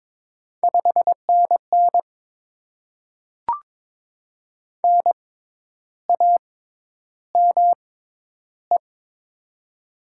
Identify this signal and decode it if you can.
Morse code, heard as a keyed tone, about 22 wpm, spaced as for 10 wpm, 700 Hz. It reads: 5NN NAME
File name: Top 20 QSO Elements - 6x Character Spacing - 22wpm